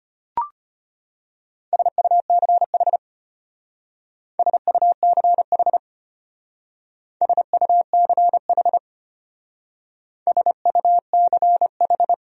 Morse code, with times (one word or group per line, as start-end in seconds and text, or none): 1.7-3.0 SUCH
4.4-5.8 SUCH
7.2-8.8 SUCH
10.3-12.1 SUCH